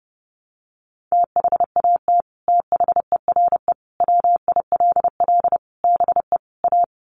Morse code, heard as a keyed tone, about 30 wpm, 700 Hz. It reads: THAT THERE WILL BE A
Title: Sets of 5 Words Encore2 30wpm